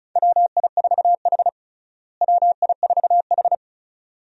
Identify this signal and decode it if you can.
Morse code, heard as a keyed tone, about 34 wpm, 700 Hz. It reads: WI4H WI4H
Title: USA Call Signs - 4 Characters - Repeated 2x - 35wpm